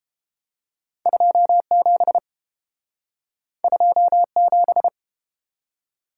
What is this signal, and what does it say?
Morse code, heard as a keyed tone, about 32 wpm, 700 Hz. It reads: 27 27